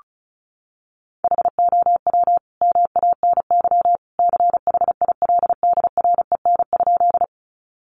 Morse code, heard as a keyed tone, about 34 wpm, 700 Hz. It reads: HOW MANY CHILDREN?
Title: Sentences from Top 300 Words 35wpm